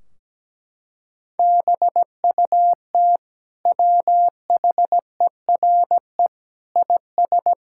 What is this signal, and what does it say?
Morse code, heard as a keyed tone, about 17 wpm, 700 Hz. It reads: BUT WHERE IS
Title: Sentences from Top 1600 Words 17wpm